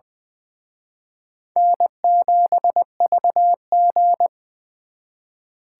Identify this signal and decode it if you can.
Morse code, heard as a keyed tone, about 24 wpm, 700 Hz. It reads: N7VG
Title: Speed Racing - USA 1x2 Call Signs 20wpm